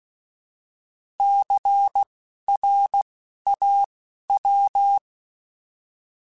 Morse code, heard as a keyed tone, about 16 wpm, spaced as for 12 wpm, 800 Hz. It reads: CRAW
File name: Alphabetical / Alphabetical mp3s / Cs @10